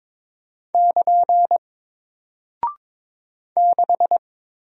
Morse code, heard as a keyed tone, about 21 wpm, 700 Hz. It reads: <KN> 6
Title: Single Character (Letter:Number:Punctuation:Prosign) Alt 22wpm